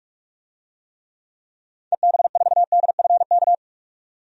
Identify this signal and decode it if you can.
Morse code, heard as a keyed tone, about 48 wpm, 700 Hz. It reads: EB4DFK
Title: DXCC-100 Call Signs - 6 Characters - 45wpm